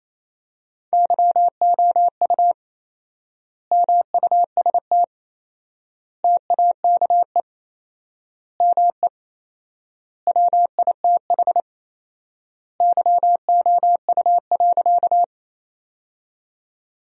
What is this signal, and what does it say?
Morse code, heard as a keyed tone, about 28 wpm, 700 Hz. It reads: YOU MUST TAKE ME WITH YOU.
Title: 4x Word-Spacing - Sentences from Top 200 Words - 28wpm